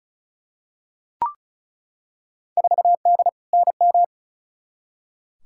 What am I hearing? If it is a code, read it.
Morse code, heard as a keyed tone, about 36 wpm, 700 Hz. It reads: 4D NM
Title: Field Day Exchanges Encore - 35wpm